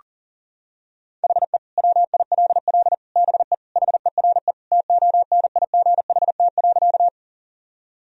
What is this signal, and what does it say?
Morse code, heard as a keyed tone, about 40 wpm, 700 Hz. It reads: HE WILL BE HERE TONIGHT.
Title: Sentences from Top 1000 Words - Review Part 3 - 40wpm